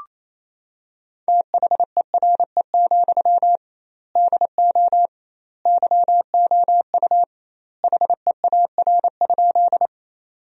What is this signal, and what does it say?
Morse code, heard as a keyed tone, about 28 wpm, 700 Hz. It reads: THERE, DO YOU HEAR?
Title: 2x Word-Spacing - Sentences from Top 300 Words - 28wpm